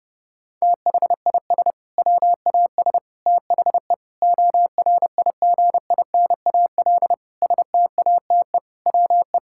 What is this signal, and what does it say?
Morse code, heard as a keyed tone, about 28 wpm, 700 Hz. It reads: THIS WAS THE ORIGINAL STATE WE
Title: Sentences from Top 1000 Words - Review Part 6 - 30wpm